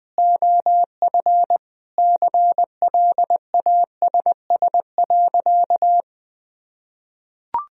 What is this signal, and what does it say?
Morse code, heard as a keyed tone, about 20 wpm, 700 Hz. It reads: OF CLASS.
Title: Sentences from Top 1000 Words - Review Part 2 - 20wpm